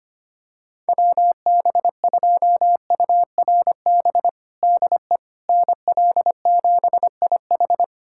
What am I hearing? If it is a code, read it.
Morse code, heard as a keyed tone, about 25 wpm, 700 Hz. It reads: WB2URB DE NL7IH
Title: USA Callsign DE Callsign 25wpm